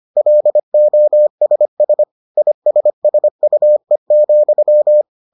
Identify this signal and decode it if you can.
Morse code, heard as a keyed tone, about 25 wpm, 600 Hz. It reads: LOSS ISSUE,